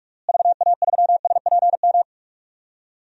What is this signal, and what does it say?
Morse code, heard as a keyed tone, about 41 wpm, 700 Hz. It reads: VA3SPM
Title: DXCC-100 Call Signs - 6 Characters - Repeated 3x - 45wpm